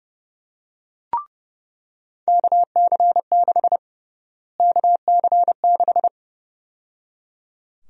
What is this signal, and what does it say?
Morse code, heard as a keyed tone, about 30 wpm, 700 Hz. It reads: KC6 KC6
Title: USA Call Sign Prefixes - Repeated 2x - 30wpm